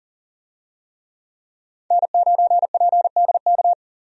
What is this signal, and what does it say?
Morse code, heard as a keyed tone, about 40 wpm, 700 Hz. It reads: N9PDK
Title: USA Call Signs - 5 Characters - 40wpm